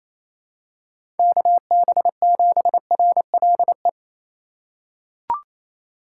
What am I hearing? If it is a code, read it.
Morse code, heard as a keyed tone, about 28 wpm, 700 Hz. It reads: KB7RLE
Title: USA Call Signs - 6 Characters - 4x Character Spacing - 28wpm